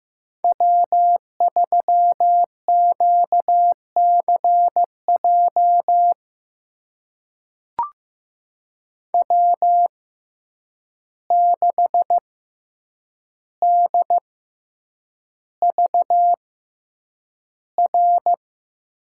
Morse code, heard as a keyed tone, about 15 wpm, spaced as for 8 wpm, 700 Hz. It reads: W3QCJ W6DVR
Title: USA Call Signs - 5 Characters - 6x Character Spacing - 15wpm